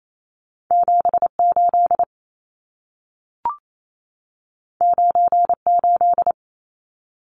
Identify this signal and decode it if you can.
Morse code, heard as a keyed tone, about 28 wpm, 700 Hz. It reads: 78 98
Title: Numbers - 2-Digit 28wpm